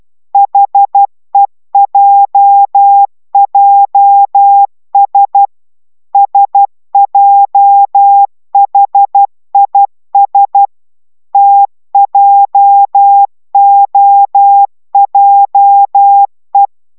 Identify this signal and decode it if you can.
Morse code, heard as a keyed tone, about 12 wpm, 800 Hz. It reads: HEJJS SJHIS TJOJE